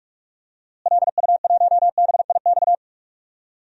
Morse code, heard as a keyed tone, about 44 wpm, 700 Hz. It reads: LU1BIX
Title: DXCC-100 Call Signs - 6 Characters - Repeated 3x - 45wpm